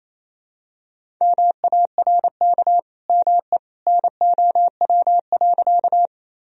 Morse code, heard as a keyed tone, about 28 wpm, 700 Hz. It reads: MARK ME NOW.
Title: Sentences from Top 1000 Words - Review Part 1 - 28wpm